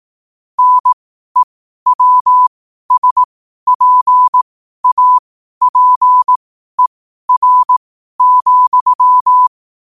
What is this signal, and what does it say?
Morse code, heard as a keyed tone, about 18 wpm, spaced as for 12 wpm, 1000 Hz. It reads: NEWSPAPER,
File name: qotdfiles / 12wpm